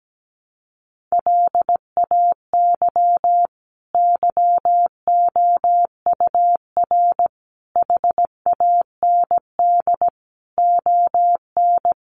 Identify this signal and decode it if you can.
Morse code, heard as a keyed tone, about 17 wpm, 700 Hz. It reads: LAY YOUR HAND ON
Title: Sentences from Top 400 Words 17wpm